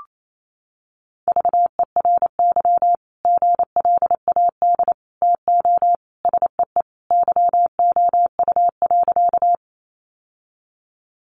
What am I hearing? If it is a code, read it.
Morse code, heard as a keyed tone, about 28 wpm, 700 Hz. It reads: VERY GLAD TO SEE YOU.
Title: Sentences from Top 800 Words 28wpm